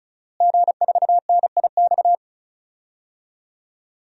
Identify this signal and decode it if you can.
Morse code, heard as a keyed tone, about 36 wpm, 700 Hz. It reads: G4NIX
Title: DXCC-100 Call Signs - 5 Characters - Repeated 3x - 35wpm